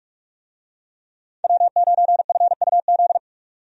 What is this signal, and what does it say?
Morse code, heard as a keyed tone, about 41 wpm, 700 Hz. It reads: W9FUZ
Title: USA Call Signs - 5 Characters - 45wpm